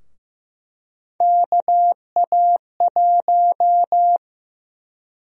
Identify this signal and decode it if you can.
Morse code, heard as a keyed tone, about 15 wpm, 700 Hz. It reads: KA1
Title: USA Call Sign Prefixes - Repeated 2x - 15wpm